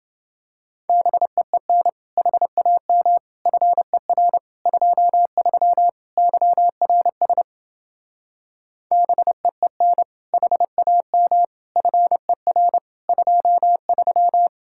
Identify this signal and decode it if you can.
Morse code, heard as a keyed tone, about 29 wpm, 700 Hz. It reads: BEEN HAM FER 23 YRS BEEN HAM FER 23